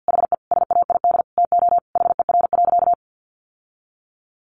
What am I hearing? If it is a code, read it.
Morse code, heard as a keyed tone, about 50 wpm, 700 Hz. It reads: HE SAID TO HER.
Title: Sentences from Top 1500 Words 50wpm